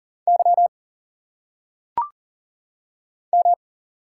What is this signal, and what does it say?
Morse code, heard as a keyed tone, about 38 wpm, 700 Hz. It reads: Y M
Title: Single Letters - Phonetic Pronunciation 40wpm